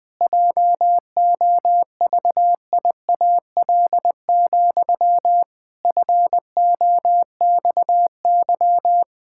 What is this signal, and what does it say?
Morse code, heard as a keyed tone, about 20 wpm, 700 Hz. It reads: JOVIAL, FOXY
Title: Pangram Challenge v2 - 20wpm